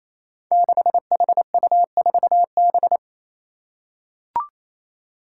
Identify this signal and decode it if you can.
Morse code, heard as a keyed tone, about 28 wpm, 700 Hz. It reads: 6HU4B
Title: Five Challenging Letter-Number Combinations 28wpm